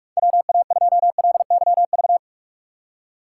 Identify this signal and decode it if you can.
Morse code, heard as a keyed tone, about 46 wpm, 700 Hz. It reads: WA2LYV
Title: USA Call Signs - 6 Characters - 6x Character Spacing - 45wpm